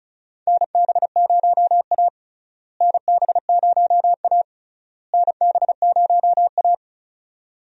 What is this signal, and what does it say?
Morse code, heard as a keyed tone, about 36 wpm, 700 Hz. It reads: NB0A NB0A NB0A
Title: USA Call Signs - 4 Characters - Repeated 3x - 35wpm